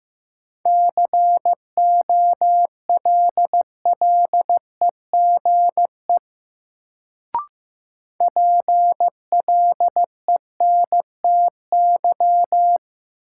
Morse code, heard as a keyed tone, about 15 wpm, 700 Hz. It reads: COLLEGE PLENTY
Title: Sentences from Top 1300 Words - New Word Drill Alt - 15wpm